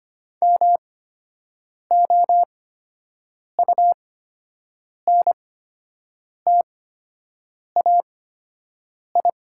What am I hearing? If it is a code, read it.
Morse code, heard as a keyed tone, about 25 wpm, spaced as for 7 wpm, 700 Hz. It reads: MOUNTAI
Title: 8x Character Spacing - Top 500 Words - 25wpm